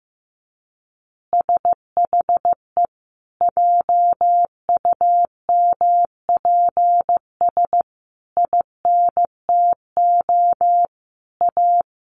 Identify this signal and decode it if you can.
Morse code, heard as a keyed tone, about 15 wpm, 700 Hz. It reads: SHE JUMPS INTO A
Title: Easy Phrases 15wpm